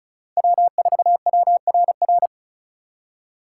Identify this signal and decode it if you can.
Morse code, heard as a keyed tone, about 44 wpm, 700 Hz. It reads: W4WRR
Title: Speed Racing - USA 1x3 Call Signs 35wpm